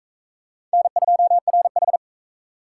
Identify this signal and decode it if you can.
Morse code, heard as a keyed tone, about 44 wpm, 700 Hz. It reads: N2RH